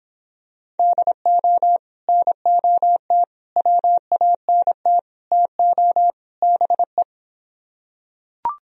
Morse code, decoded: DO NOT WANT TO BE

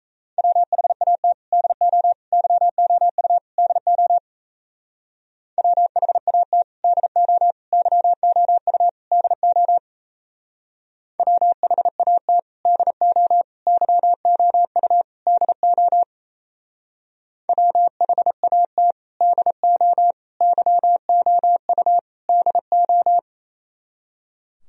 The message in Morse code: WHAT DO YOU DO WHAT DO YOU DO WHAT DO YOU DO WHAT DO YOU DO

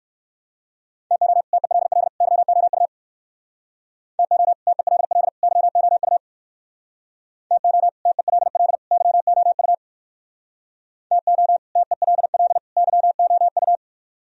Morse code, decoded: TO TELL YOU TO TELL YOU TO TELL YOU TO TELL YOU

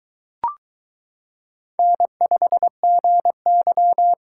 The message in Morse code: N5GY

